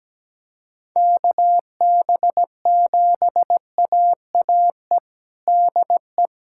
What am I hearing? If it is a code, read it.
Morse code, heard as a keyed tone, about 17 wpm, 700 Hz. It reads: KB7AAE DE
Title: USA Callsign DE Callsign - 6 Characters - 17wpm